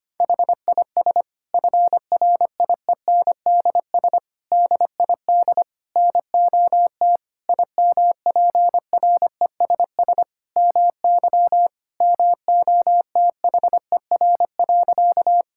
Morse code, heard as a keyed tone, about 25 wpm, 700 Hz. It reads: HIS FRIENDS DID NOT IMPRESS MY MOTHER.
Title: Sentences from Top 1600 Words 25wpm